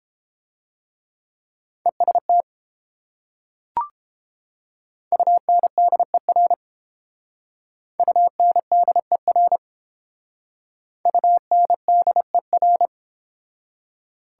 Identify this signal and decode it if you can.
Morse code, heard as a keyed tone, about 30 wpm, 700 Hz. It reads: EST UNDER UNDER UNDER